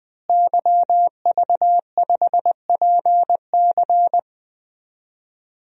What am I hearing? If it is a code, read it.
Morse code, heard as a keyed tone, about 20 wpm, 700 Hz. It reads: YV5PC
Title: DXCC-100 Call Signs - 5 Characters - Repeated 3x - 20wpm